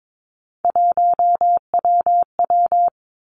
Morse code, heard as a keyed tone, about 22 wpm, 700 Hz. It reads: 1WW
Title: Three Challenging Letter-Number Combinations 22wpm